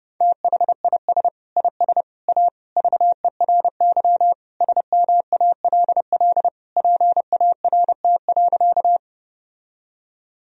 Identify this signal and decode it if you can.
Morse code, heard as a keyed tone, about 32 wpm, 700 Hz. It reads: THIS IS A VERY SMALL PART.
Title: Sentences from Top 1000 Words - Review Part 1 - 30wpm